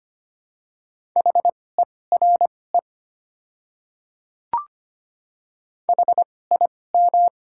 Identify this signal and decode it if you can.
Morse code, heard as a keyed tone, about 25 wpm, spaced as for 18 wpm, 700 Hz. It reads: HERE HIM
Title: 2x Character Spacing - Top 200 Words - 25wpm